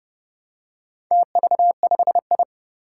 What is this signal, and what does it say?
Morse code, heard as a keyed tone, about 30 wpm, 700 Hz. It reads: TV5I
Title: Four Challenging Letter-Number Combinations 30wpm